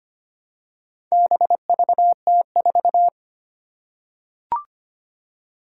Morse code, heard as a keyed tone, about 25 wpm, 700 Hz. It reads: BVT4
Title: Four Challenging Letter-Number Combinations 25wpm